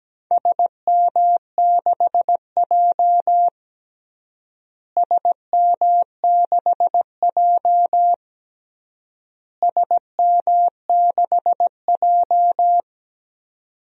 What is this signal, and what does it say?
Morse code, heard as a keyed tone, about 17 wpm, 700 Hz. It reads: SM6J SM6J SM6J
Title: DXCC-100 Call Signs - 4 Characters - Repeated 3x - 17wpm